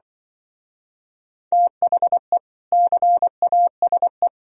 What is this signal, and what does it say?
Morse code, heard as a keyed tone, about 25 wpm, 700 Hz. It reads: THE CASE
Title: Speed-Racing - Sets of 2 Words 20wpm